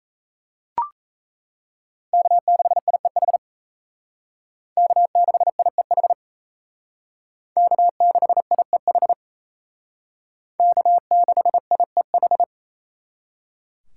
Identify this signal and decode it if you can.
Morse code, heard as a keyed tone, about 37 wpm, 700 Hz. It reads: K6IEH K6IEH K6IEH K6IEH